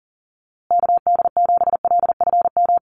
Morse code, heard as a keyed tone, about 40 wpm, 700 Hz. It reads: KD7LFM
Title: USA Call Signs Repeated 2x 40wpm